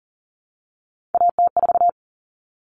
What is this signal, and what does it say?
Morse code, heard as a keyed tone, about 40 wpm, 700 Hz. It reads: AT4
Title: Common Cut Numbers - 3-Digit 40wpm